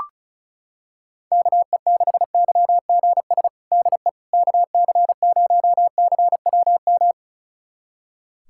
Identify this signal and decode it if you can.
Morse code, heard as a keyed tone, about 36 wpm, 700 Hz. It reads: KE6YGS DE KC0CWM
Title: USA Callsign DE Callsign 35wpm